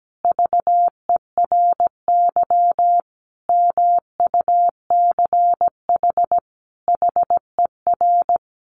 Morse code, heard as a keyed tone, about 17 wpm, 700 Hz. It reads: VERY MUCH HER
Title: Sentences from Top 1200 Words 17wpm